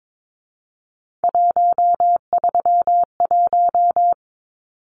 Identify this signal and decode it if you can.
Morse code, heard as a keyed tone, about 22 wpm, 700 Hz. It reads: 131
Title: Numbers - 3-Digit 22wpm